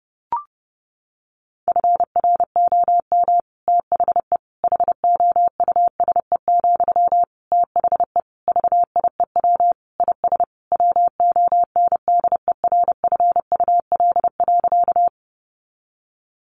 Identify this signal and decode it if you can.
Morse code, heard as a keyed tone, about 30 wpm, 700 Hz. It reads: FROM THE HOUSE, THE VIEW IS WONDERFUL.
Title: Sentences from Top 1100 Words 30wpm